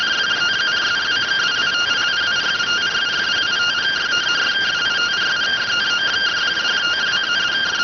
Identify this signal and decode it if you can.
Two-tone teleprinter signal, about 170 Hz shift.